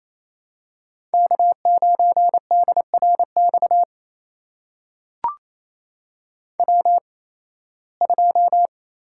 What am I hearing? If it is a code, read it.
Morse code, heard as a keyed tone, about 28 wpm, spaced as for 18 wpm, 700 Hz. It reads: K9DRX W2
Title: USA Call Signs - 5 Characters - 8x Character Spacing - 28wpm